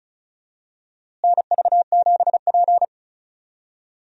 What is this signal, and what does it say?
Morse code, heard as a keyed tone, about 36 wpm, 700 Hz. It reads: NV7P
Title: USA Call Signs - 4 Characters - Repeated 3x - 35wpm